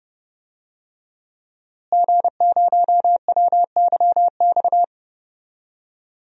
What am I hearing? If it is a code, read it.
Morse code, heard as a keyed tone, about 30 wpm, 700 Hz. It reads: G0WYX
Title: DXCC-100 Call Signs - 5 Characters - Repeated 2x - 30wpm